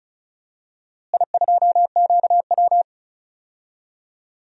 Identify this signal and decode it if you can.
Morse code, heard as a keyed tone, about 36 wpm, 700 Hz. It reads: I2QW
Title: DXCC-100 Call Signs - 4 Characters - Repeated 3x - 35wpm